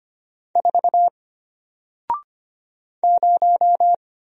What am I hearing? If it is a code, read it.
Morse code, heard as a keyed tone, about 25 wpm, 700 Hz. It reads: 4 0
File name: Single-Letter-Number-25wpm